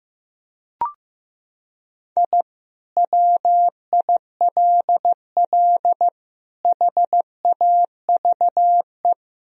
Morse code, heard as a keyed tone, about 15 wpm, 700 Hz. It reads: I WILL HAVE
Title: Sentences from Top 1000 Words - Review Part 4 - 15wpm